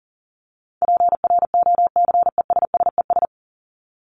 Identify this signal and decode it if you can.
Morse code, heard as a keyed tone, about 40 wpm, 700 Hz. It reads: PROCESSES
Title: ES:S Suffix - Top 100 Words with Base Words 40wpm